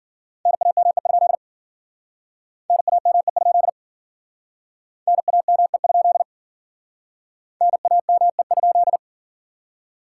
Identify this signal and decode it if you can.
Morse code, heard as a keyed tone, about 51 wpm, 700 Hz. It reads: NAME? NAME? NAME? NAME?